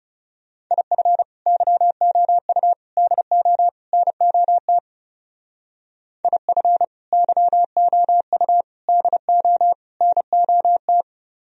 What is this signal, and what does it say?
Morse code, heard as a keyed tone, about 36 wpm, 700 Hz. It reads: IF YOU DO NOT IF YOU DO NOT
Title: Speed-Racing - Sets of 4 Words 30wpm